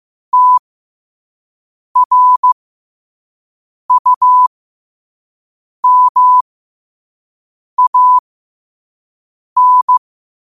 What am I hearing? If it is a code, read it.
Morse code, heard as a keyed tone, about 15 wpm, spaced as for 5 wpm, 1000 Hz. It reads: TRUMAN
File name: qotdfiles / 5wpm